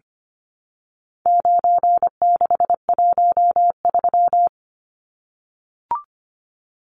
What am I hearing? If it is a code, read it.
Morse code, heard as a keyed tone, about 25 wpm, 700 Hz. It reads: 9613
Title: Numbers - 4-Digit 25wpm